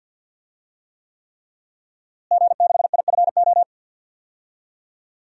Morse code, heard as a keyed tone, about 52 wpm, 700 Hz. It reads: G6IFO
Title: DXCC-100 Call Signs - 5 Characters - Repeated 3x - 50wpm